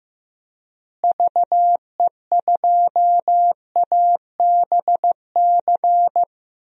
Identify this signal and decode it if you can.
Morse code, heard as a keyed tone, about 15 wpm, 700 Hz. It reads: VE2ABC
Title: DXCC-100 Call Signs - 6 Characters - Repeated 3x - 15wpm